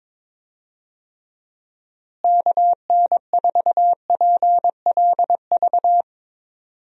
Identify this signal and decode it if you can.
Morse code, heard as a keyed tone, about 22 wpm, 700 Hz. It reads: KN4PLV
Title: USA Call Signs - 6 Characters - Repeated 3x - 22wpm